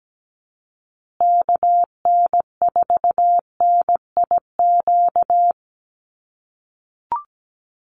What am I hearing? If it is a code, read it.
Morse code, heard as a keyed tone, about 17 wpm, 700 Hz. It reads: KN4NIQ
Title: USA Call Signs Repeated 3x 17wpm